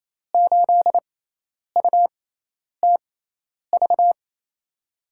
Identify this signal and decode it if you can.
Morse code, heard as a keyed tone, about 28 wpm, spaced as for 10 wpm, 700 Hz. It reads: 8UTV
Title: USA Call Signs - 6 Characters - 6x Character Spacing - 28wpm